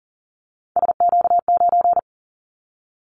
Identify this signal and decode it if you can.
Morse code, heard as a keyed tone, about 40 wpm, 700 Hz. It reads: SQ9
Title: Callsign Prefixes for Top 25 DXCC - Repeated 3x - 40wpm